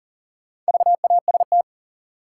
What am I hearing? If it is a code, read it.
Morse code, heard as a keyed tone, about 38 wpm, 700 Hz. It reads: VAST
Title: Sets of 4 Words - New Word Drill - 40wpm